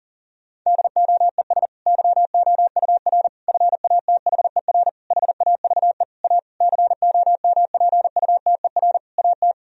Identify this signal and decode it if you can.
Morse code, heard as a keyed tone, about 37 wpm, 700 Hz. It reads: DOES YOUR FATHER HAVE A COMPUTER AT
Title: Sentences from Top 1000 Words - Review Part 3 - 40wpm